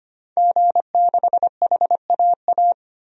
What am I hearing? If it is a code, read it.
Morse code, heard as a keyed tone, about 25 wpm, 700 Hz. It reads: G6HAA
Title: DXCC-100 Call Signs - 5 Characters - 25wpm